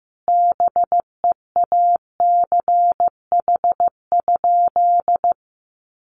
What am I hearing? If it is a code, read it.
Morse code, heard as a keyed tone, about 15 wpm, 700 Hz. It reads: BEACH?
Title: Sentences from Top 1500 Words 15wpm